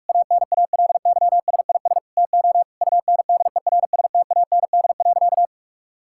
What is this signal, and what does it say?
Morse code, heard as a keyed tone, about 41 wpm, 700 Hz. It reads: ANALYSIS TO UNDERSTAND.
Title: Sentences from Top 1000 Words - Review Part 5 - 45wpm